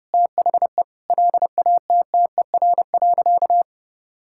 Morse code, heard as a keyed tone, about 30 wpm, 700 Hz. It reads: THE LATTER.
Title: Sentences from Top 1600 Words 30wpm